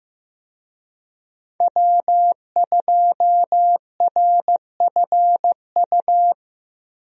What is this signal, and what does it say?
Morse code, heard as a keyed tone, about 15 wpm, 700 Hz. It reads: W2RFU
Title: USA Call Signs - 5 Characters - 15wpm